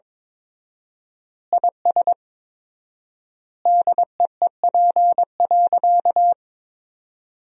Morse code, heard as a keyed tone, about 22 wpm, 700 Hz. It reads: IS DEEP.
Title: 4x Word-Spacing - Sentences from Top 500 Words - 22wpm